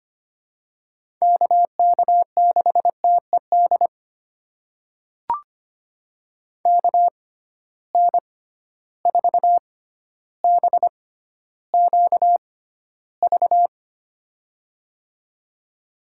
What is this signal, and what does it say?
Morse code, heard as a keyed tone, about 25 wpm, spaced as for 13 wpm, 700 Hz. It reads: KK6TED KN4BQV
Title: USA Call Signs - 6 Characters - 6x Character Spacing - 25wpm